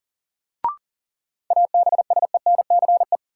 Morse code, heard as a keyed tone, about 40 wpm, 700 Hz. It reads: ABSENCE
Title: Sets of 4 Words - New Word Drill - 40wpm